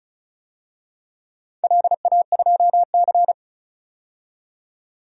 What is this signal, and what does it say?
Morse code, heard as a keyed tone, about 36 wpm, 700 Hz. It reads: LA2C
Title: DXCC-100 Call Signs - 4 Characters - 35wpm